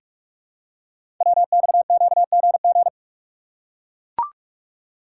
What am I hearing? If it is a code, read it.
Morse code, heard as a keyed tone, about 44 wpm, 700 Hz. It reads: WXQGG
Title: Code Groups - 5 Letters - 45wpm